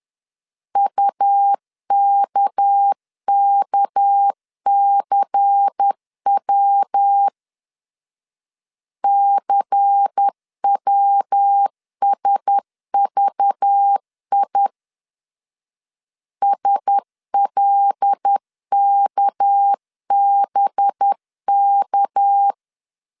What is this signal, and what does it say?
Morse code, heard as a keyed tone, about 10 wpm, 800 Hz. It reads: UKKCW CWSVI SLKBK
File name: LEZIONI / 6 PKX0 / 6 PK 10wpm 2436